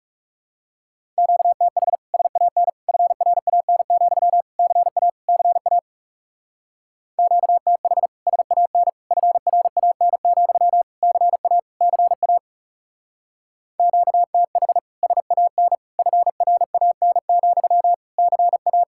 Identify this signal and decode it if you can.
Morse code, heard as a keyed tone, about 36 wpm, 700 Hz. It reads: QTH SAN FRAN, CA CA QTH SAN FRAN, CA CA QTH SAN FRAN, CA
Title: Speed-Racing Common QSO Phrases - 30wpm